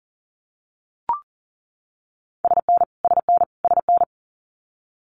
Morse code, heard as a keyed tone, about 38 wpm, 700 Hz. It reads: SN SN SN
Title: Multi-letter Phonemes Repeated 3x 40wpm